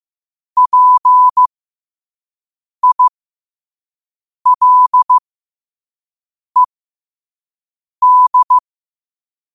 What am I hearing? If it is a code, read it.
Morse code, heard as a keyed tone, about 15 wpm, spaced as for 5 wpm, 1000 Hz. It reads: PILED
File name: qotdfiles / 5wpm